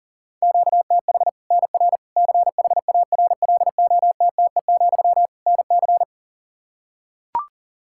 Morse code, heard as a keyed tone, about 40 wpm, 700 Hz. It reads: QTH NR CHARLOTTE, NC
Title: Practice QSO - Exchange 1 - Beginning - 40wpm